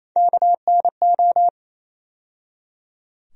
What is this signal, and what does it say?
Morse code, heard as a keyed tone, about 28 wpm, 700 Hz. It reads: KNO
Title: USA Call Sign Postfixes - 3 Letters 28wpm